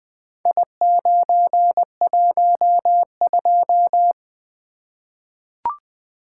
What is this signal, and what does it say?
Morse code, heard as a keyed tone, about 20 wpm, 700 Hz. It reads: I912
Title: Four Letter-Number Combinations - 20wpm